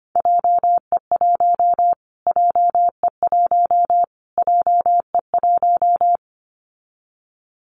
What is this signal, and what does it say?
Morse code, heard as a keyed tone, about 25 wpm, 700 Hz. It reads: JE1 JE1 JE1
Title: Callsign Prefixes for Top 25 DXCC - Repeated 3x - 25wpm